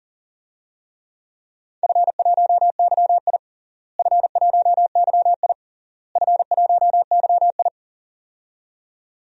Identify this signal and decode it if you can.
Morse code, heard as a keyed tone, about 40 wpm, 700 Hz. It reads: F1YI F1YI F1YI
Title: DXCC-100 Call Signs - 4 Characters - Repeated 3x - 40wpm